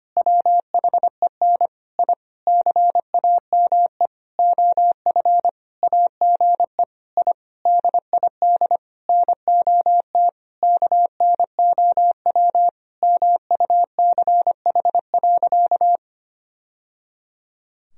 Morse code, heard as a keyed tone, about 25 wpm, 700 Hz. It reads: WHEN I CAME OF AGE I DID NOT KNOW MUCH.
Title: Sentences from Top 1000 Words - Review Part 3 - 25wpm